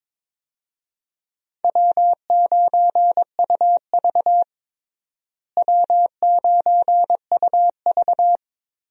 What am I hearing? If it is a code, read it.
Morse code, heard as a keyed tone, about 22 wpm, 700 Hz. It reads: W9UV W9UV
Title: USA Call Signs - 4 Characters - Repeated 2x - 22wpm